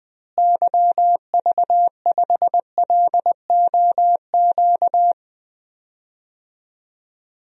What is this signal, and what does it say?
Morse code, heard as a keyed tone, about 20 wpm, 700 Hz. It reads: YV5LOQ
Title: DXCC-100 Call Signs - 6 Characters - Repeated 2x - 20wpm